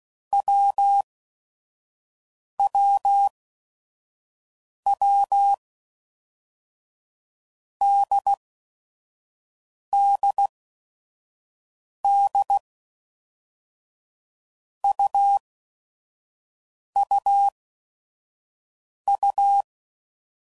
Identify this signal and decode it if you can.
Morse code, heard as a keyed tone, about 16 wpm, 800 Hz. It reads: W W W D D D U U U
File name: Basic Letter Groups By Blocks